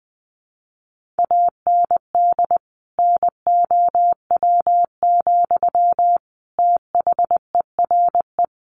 Morse code, heard as a keyed tone, about 20 wpm, 700 Hz. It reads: AND NOW, THERE
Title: Sentences from Top 1500 Words 20wpm